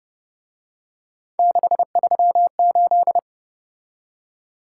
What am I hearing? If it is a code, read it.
Morse code, heard as a keyed tone, about 30 wpm, 700 Hz. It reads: 638